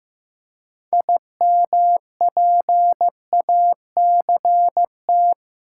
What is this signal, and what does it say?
Morse code, heard as a keyed tone, about 15 wpm, 700 Hz. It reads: IMPACT
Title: Six Letter Words - 15wpm